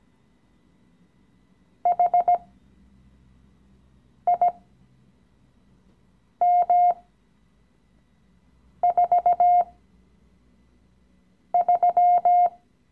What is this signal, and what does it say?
Morse code, heard as a keyed tone, about 17 wpm, spaced as for 4 wpm, 710 Hz. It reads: HIM43
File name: CW-Online-Kurs / Anfänger / Lektion 015